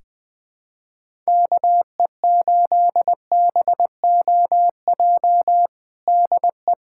Morse code, heard as a keyed tone, about 20 wpm, 700 Hz. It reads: KE8BOJ DE
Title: USA Callsign DE Callsign - 6 Characters - 20wpm